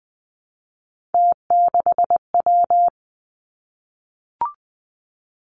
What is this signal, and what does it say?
Morse code, heard as a keyed tone, about 19 wpm, 700 Hz. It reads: T6W
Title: Three Challenging Letter-Number Combinations 20wpm